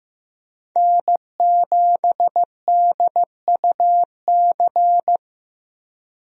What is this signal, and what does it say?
Morse code, heard as a keyed tone, about 15 wpm, 700 Hz. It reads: N7DUC
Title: USA Call Signs - 5 Characters - Repeated 3x - 15wpm